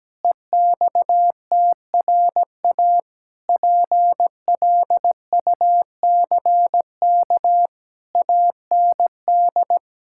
Morse code, heard as a keyed tone, about 17 wpm, 700 Hz. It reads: EXTRA PLUCK AND
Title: Pangram Challenge v2 - 17wpm